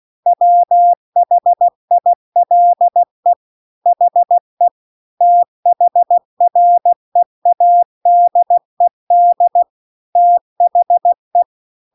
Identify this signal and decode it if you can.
Morse code, heard as a keyed tone, about 16 wpm, 700 Hz. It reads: WHILE HE THREADED THE